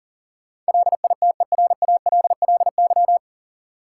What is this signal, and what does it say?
Morse code, heard as a keyed tone, about 37 wpm, 700 Hz. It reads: LITERALLY
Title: Sentences from Top 1400 Words - New Word Drill - 40wpm